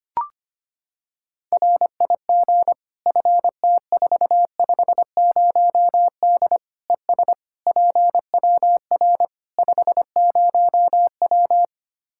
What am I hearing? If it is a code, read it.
Morse code, heard as a keyed tone, about 25 wpm, 700 Hz. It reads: RIG FT450D ES PWR 50W
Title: K1USN SST QSO Practice 25wpm